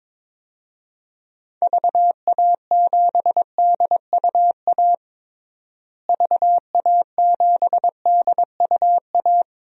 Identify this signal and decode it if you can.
Morse code, heard as a keyed tone, about 22 wpm, 700 Hz. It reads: VA7DUA VA7DUA
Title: DXCC-100 Call Signs - 6 Characters - Repeated 2x - 22wpm